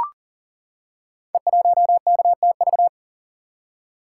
Morse code, heard as a keyed tone, about 38 wpm, 700 Hz. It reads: E1KTV
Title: Code Groups - 5 Letter:Numbers - 40wpm